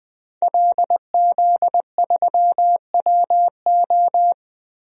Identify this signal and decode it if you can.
Morse code, heard as a keyed tone, about 21 wpm, 700 Hz. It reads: LZ3WO
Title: DXCC-100 Call Signs - 5 Characters - Repeated 2x - 20wpm